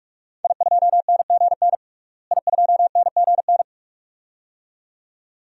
Morse code, heard as a keyed tone, about 46 wpm, 700 Hz. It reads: I2NGN I2NGN
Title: DXCC-100 Call Signs - 5 Characters - Repeated 2x - 45wpm